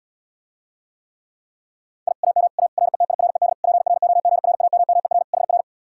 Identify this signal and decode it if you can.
Morse code, heard as a keyed tone, about 77 wpm, 700 Hz. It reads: I AM A RETIRED PROGRAMMER HR